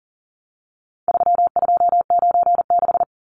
Morse code, heard as a keyed tone, about 40 wpm, 700 Hz. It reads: 3296
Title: Numbers - 4-Digit 40wpm